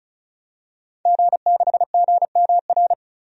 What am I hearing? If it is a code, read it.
Morse code, heard as a keyed tone, about 34 wpm, 700 Hz. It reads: G6GMR
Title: DXCC-100 Call Signs - 5 Characters - Repeated 2x - 35wpm